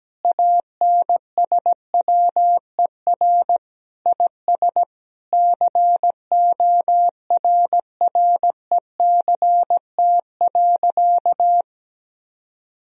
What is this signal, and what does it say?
Morse code, heard as a keyed tone, about 17 wpm, 700 Hz. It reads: ANSWER IS CORRECT.
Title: Sentences from Top 1000 Words - Review Part 2 - 17wpm